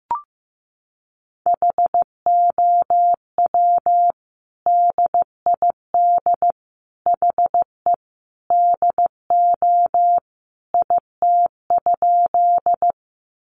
Morse code, heard as a keyed tone, about 15 wpm, 700 Hz. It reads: HOW DID HE DO IT?
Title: Sentences from Top 100 Words 15wpm